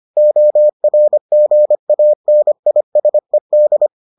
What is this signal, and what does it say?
Morse code, heard as a keyed tone, about 25 wpm, 600 Hz. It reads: ORGANISED